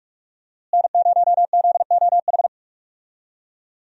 Morse code, heard as a keyed tone, about 46 wpm, 700 Hz. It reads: N0ZOH
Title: USA Call Signs - 5 Characters - Repeated 2x - 45wpm